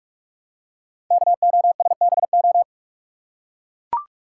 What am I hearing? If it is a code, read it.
Morse code, heard as a keyed tone, about 44 wpm, 700 Hz. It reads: KOSBO
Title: Code Groups - 5 Letters - 45wpm